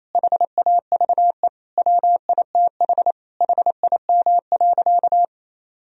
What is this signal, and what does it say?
Morse code, heard as a keyed tone, about 28 wpm, 700 Hz. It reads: HAVE WITH HIM.
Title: Sentences from Top 1000 Words - Review Part 1 - 28wpm